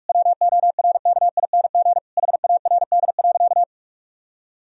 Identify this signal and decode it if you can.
Morse code, heard as a keyed tone, about 46 wpm, 700 Hz. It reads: WORKING HARD.
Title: Sentences from Top 1000 Words - Review Part 4 - 45wpm